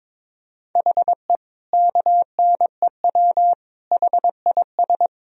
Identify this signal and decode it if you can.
Morse code, heard as a keyed tone, about 21 wpm, 700 Hz. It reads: HE KNEW HIS
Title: Sentences from Top 1000 Words - Review Part 1 - 22wpm